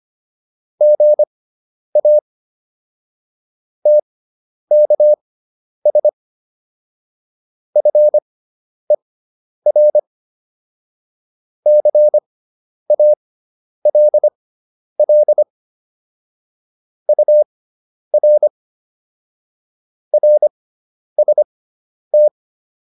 Morse code, heard as a keyed tone, about 25 wpm, spaced as for 10 wpm, 600 Hz. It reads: GA TKS FER CALL UR RST
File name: QSO404 10